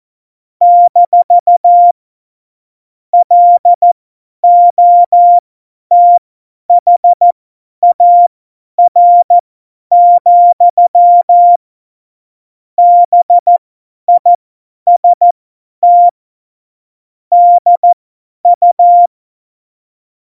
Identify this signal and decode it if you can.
Morse code, heard as a keyed tone, about 14 wpm, spaced as for 10 wpm, 700 Hz. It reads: - LOTHAR, BIST DU
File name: Morsekurs / Witze / Witz 20. August 2019, 50 F70